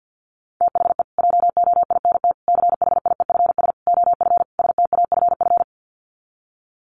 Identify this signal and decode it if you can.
Morse code, heard as a keyed tone, about 50 wpm, 700 Hz. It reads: THE JOINT CHIEFS OF STAFF
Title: Sets of 5 Words Encore2 50wpm